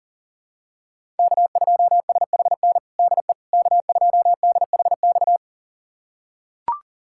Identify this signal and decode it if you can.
Morse code, heard as a keyed tone, about 40 wpm, 700 Hz. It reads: K2SHN DE K2DHX
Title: USA Callsign DE Callsign - 5 Characters - 40wpm